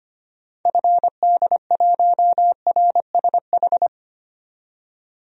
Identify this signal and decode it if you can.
Morse code, heard as a keyed tone, about 25 wpm, 690 Hz. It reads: FD1RSH
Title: DXCC-100 Call Signs - 6 Characters - 25wpm